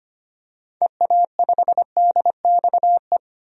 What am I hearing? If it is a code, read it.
Morse code, heard as a keyed tone, about 25 wpm, 700 Hz. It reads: EA5DXE
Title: DXCC-100 Call Signs - 6 Characters - 25wpm